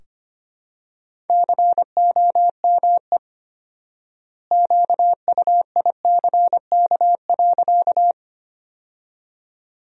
Morse code, heard as a keyed tone, about 25 wpm, 700 Hz. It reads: COME QUICK.
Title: 4x Word-Spacing - Sentences from Top 500 Words - 25wpm